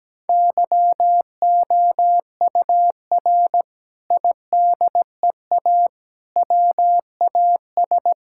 Morse code, heard as a keyed tone, about 17 wpm, 700 Hz. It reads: YOUR IDEA WAS
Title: Sentences from Top 1000 Words - Review Part 6 - 17wpm